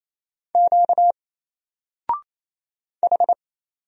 Morse code, heard as a keyed tone, about 27 wpm, 700 Hz. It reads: Q H
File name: Single-Letter-Number-28wpm